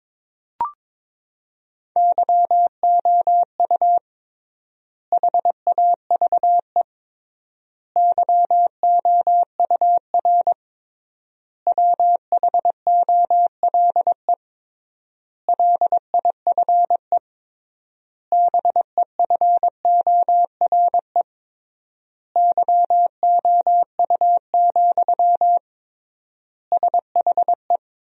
Morse code, heard as a keyed tone, about 22 wpm, 700 Hz. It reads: YOU HAVE YOUR WHOLE LIFE BEFORE YOU, SHE
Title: 3x Word-Spacing - Sentences from Top 400 Words - 22wpm